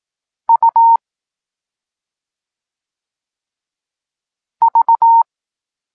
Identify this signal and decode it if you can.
Morse code, heard as a keyed tone, about 18 wpm, spaced as for 2 wpm, 920 Hz. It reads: UV
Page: Tekst 8 – doskonalenie odbioru poznanych liter